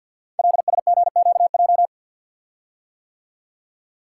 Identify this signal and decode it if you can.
Morse code, heard as a keyed tone, about 52 wpm, 700 Hz. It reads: LSGQJ
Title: Code Groups - 5 Letter:Numbers - 50wpm